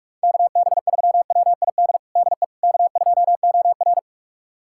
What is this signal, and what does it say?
Morse code, heard as a keyed tone, about 48 wpm, 700 Hz. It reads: KB3WID DE K2OR
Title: USA Callsign DE Callsign 45wpm